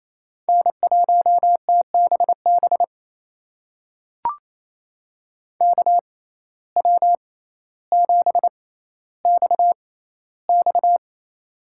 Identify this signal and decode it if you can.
Morse code, heard as a keyed tone, about 27 wpm, spaced as for 14 wpm, 700 Hz. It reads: N1TBB KW7XX
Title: USA Call Signs - 5 Characters - 6x Character Spacing - 28wpm